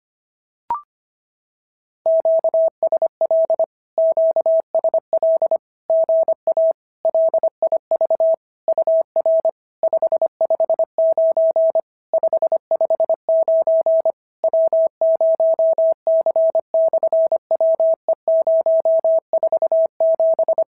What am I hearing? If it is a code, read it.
Morse code, heard as a keyed tone, about 25 wpm, 650 Hz. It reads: QSL QSL GA LIV UR 559 559 W0C/WE047
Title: SOTA QSO Practice 25wpm